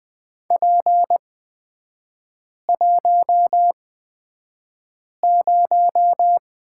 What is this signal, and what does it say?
Morse code, heard as a keyed tone, about 20 wpm, spaced as for 5 wpm, 700 Hz. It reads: P10
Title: 250 Lesson 40e - BK - Three Characters 20wpm